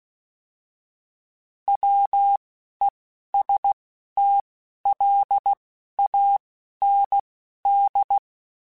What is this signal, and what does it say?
Morse code, heard as a keyed tone, about 16 wpm, spaced as for 11 wpm, 800 Hz. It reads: WESTLAND